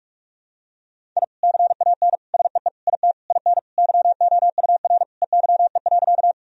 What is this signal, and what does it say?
Morse code, heard as a keyed tone, about 46 wpm, 700 Hz. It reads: I CAN SEE IT IN YOUR EYE.
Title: Sentences from Top 1000 Words - Review Part 1 - 45wpm